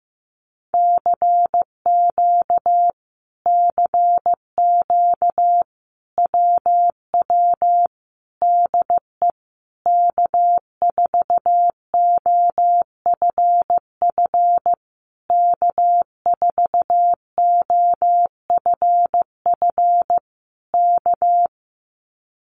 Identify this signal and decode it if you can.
Morse code, heard as a keyed tone, about 15 wpm, 700 Hz. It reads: CQ CQ WW DE K4OFF K4OFF K
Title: Common QSO Phrases Encore 15wpm